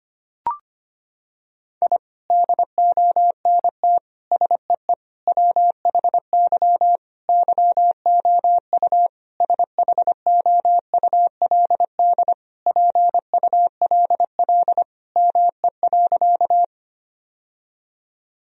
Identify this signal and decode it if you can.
Morse code, heard as a keyed tone, about 25 wpm, 700 Hz. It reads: I DONT SEE WHY YOU SHOULD PULL ME.
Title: Sentences from Top 1000 Words - Review Part 2 - 25wpm